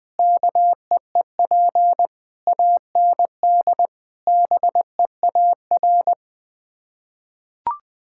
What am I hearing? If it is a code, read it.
Morse code, heard as a keyed tone, about 20 wpm, 700 Hz. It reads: KEEP AND BEAR
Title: Sets of 8 Words - 20wpm